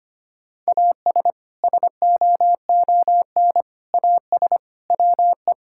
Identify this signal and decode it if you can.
Morse code, heard as a keyed tone, about 25 wpm, 700 Hz. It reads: AS SOON AS WE
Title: Sentences from Top 1600 Words 25wpm